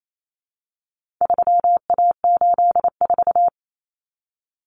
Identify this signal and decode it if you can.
Morse code, heard as a keyed tone, about 28 wpm, 700 Hz. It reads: 3A84
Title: Common Cut Numbers - 4-Digit 28wpm